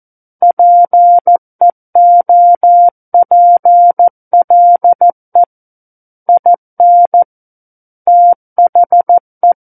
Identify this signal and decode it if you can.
Morse code, heard as a keyed tone, about 14 wpm, 700 Hz. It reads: PEOPLE IN THE
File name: Book One / 05-Chapter 5 - 14 WPM